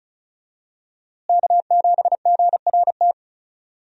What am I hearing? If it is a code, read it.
Morse code, heard as a keyed tone, about 36 wpm, 700 Hz. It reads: K7GRT
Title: USA Call Signs - 5 Characters - Repeated 3x - 35wpm